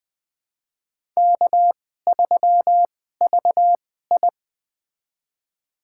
Morse code, heard as a keyed tone, about 20 wpm, spaced as for 14 wpm, 700 Hz. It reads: K3VI
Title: USA Call Signs - 4 Characters - 2x Character Spacing - 20wpm